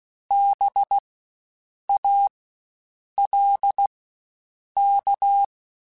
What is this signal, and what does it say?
Morse code, heard as a keyed tone, about 16 wpm, spaced as for 7 wpm, 800 Hz. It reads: BALK
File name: Spaced Alphas / Spaced Alphas mp3s / Bs sp @15